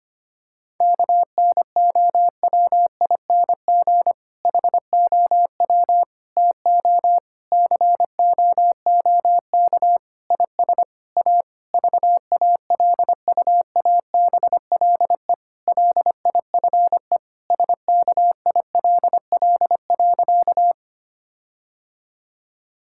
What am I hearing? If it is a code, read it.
Morse code, heard as a keyed tone, about 25 wpm, 700 Hz. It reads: KNOWING HOW TO COOK IS A VALUABLE LIFE SKILL.
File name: practice / 25wpm / Difficult Sentences with Hints - 25wpm